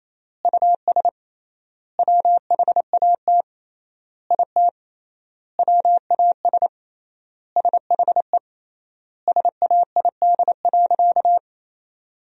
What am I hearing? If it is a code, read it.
Morse code, heard as a keyed tone, about 28 wpm, 700 Hz. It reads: US WHAT IT WAS SHE SAID.
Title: 3x Word-Spacing - Sentences from Top 400 Words - 28wpm